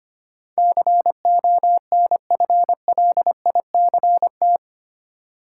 Morse code, heard as a keyed tone, about 25 wpm, 700 Hz. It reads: CONFLICT
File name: Eight Letter Words - 25wpm